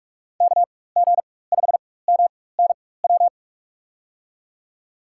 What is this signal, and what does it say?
Morse code, heard as a keyed tone, about 44 wpm, spaced as for 21 wpm, 700 Hz. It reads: KG5MNW
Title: USA Call Signs - 6 Characters - 4x Character Spacing - 45wpm